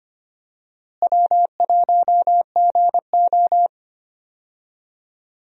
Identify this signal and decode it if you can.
Morse code, heard as a keyed tone, about 25 wpm, 700 Hz. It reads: W1GO
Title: USA Call Signs - 4 Characters - Repeated 3x - 25wpm